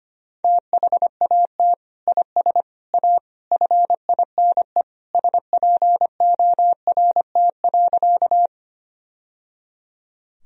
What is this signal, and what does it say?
Morse code, heard as a keyed tone, about 25 wpm, 700 Hz. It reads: THAT IS A FINE SPORT.
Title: Sentences from Top 1000 Words - Review Part 4 - 25wpm